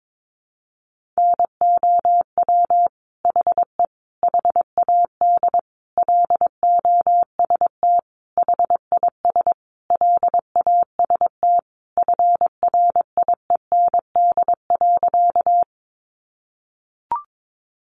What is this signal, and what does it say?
Morse code, heard as a keyed tone, about 22 wpm, 700 Hz. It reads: NOW HE HAD LOST HIS LAST FRIEND.
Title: Sentences from Top 1400 Words 22wpm